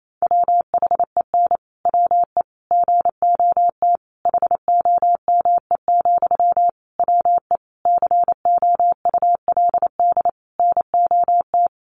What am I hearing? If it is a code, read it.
Morse code, heard as a keyed tone, about 28 wpm, 700 Hz. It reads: WHEN WE GOT HOME, WE COULD NOT
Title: Sentences from Top 1400 Words 28wpm